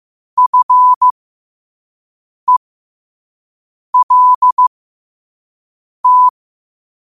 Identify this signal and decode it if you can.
Morse code, heard as a keyed tone, about 15 wpm, spaced as for 5 wpm, 1000 Hz. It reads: FELT
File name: qotdfiles / 5wpm